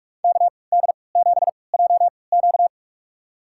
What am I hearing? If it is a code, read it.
Morse code, heard as a keyed tone, about 46 wpm, spaced as for 26 wpm, 700 Hz. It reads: KD7JQ
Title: USA Call Signs - 5 Characters - 3x Character Spacing - 45wpm